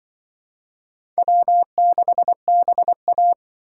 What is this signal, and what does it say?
Morse code, heard as a keyed tone, about 25 wpm, 700 Hz. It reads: W6BA
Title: Speed Racing - USA Call Signs 20wpm